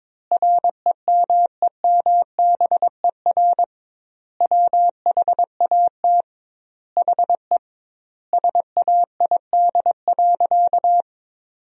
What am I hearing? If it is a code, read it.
Morse code, heard as a keyed tone, about 22 wpm, 700 Hz. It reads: REMEMBER WHAT HE SAID.
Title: 2x Word-Spacing - Sentences from Top 400 Words - 22wpm